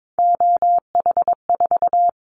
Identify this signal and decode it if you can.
Morse code, heard as a keyed tone, about 22 wpm, 700 Hz. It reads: OH4
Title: Callsign Prefixes for Top 50 DXCC - 22wpm